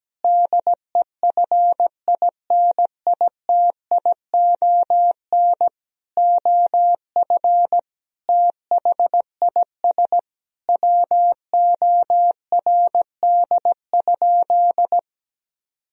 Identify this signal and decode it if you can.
Morse code, heard as a keyed tone, about 17 wpm, 700 Hz. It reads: DEFINITION OF THIS WORD?
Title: Sentences from Top 1000 Words - Review Part 4 - 17wpm